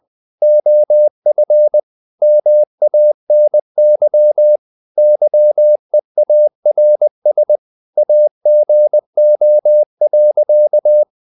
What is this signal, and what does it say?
Morse code, heard as a keyed tone, about 20 wpm, 600 Hz. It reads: OF MANY YEARS AGO.
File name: the old man and the sea / the old man and the sea part 1 20wpm